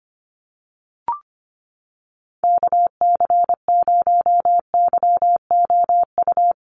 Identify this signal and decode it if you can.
Morse code, heard as a keyed tone, about 25 wpm, 700 Hz. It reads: KC0YOU
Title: USA Call Signs 25wpm